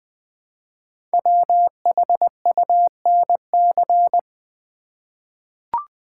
Morse code, heard as a keyed tone, about 20 wpm, 700 Hz. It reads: WHUNC
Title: Code Groups - 5 Letters - 20wpm